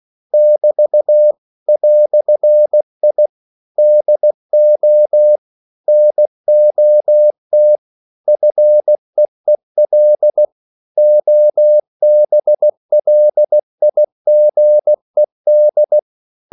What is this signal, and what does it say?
Morse code, heard as a keyed tone, about 16 wpm, 600 Hz. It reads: <BT> "I DO NOT FEEL OBLIGED
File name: CW / Practice / EnglishQuotations0116